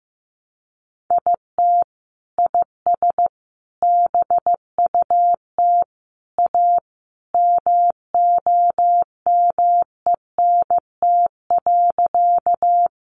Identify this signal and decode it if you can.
Morse code, heard as a keyed tone, about 15 wpm, 700 Hz. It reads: IT IS BUT A MOMENT.
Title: Sentences from Top 1000 Words 15wpm